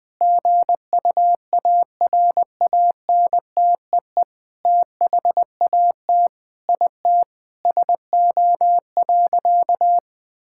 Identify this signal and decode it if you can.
Morse code, heard as a keyed tone, about 21 wpm, 700 Hz. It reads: GUARANTEE THAT IT SO.
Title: Sentences from Top 1600 Words 20wpm